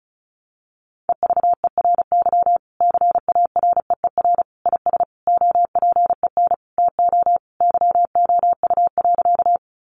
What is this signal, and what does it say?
Morse code, heard as a keyed tone, about 34 wpm, 700 Hz. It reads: EVERY CAREER IS OPEN TO YOU.
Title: Sentences from Top 1000 Words 35wpm